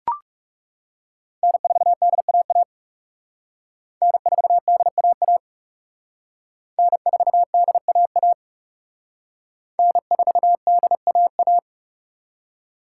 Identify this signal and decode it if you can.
Morse code, heard as a keyed tone, about 35 wpm, 700 Hz. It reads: N4DAA N4DAA N4DAA N4DAA
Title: Speed Racing - USA Call Signs 30wpm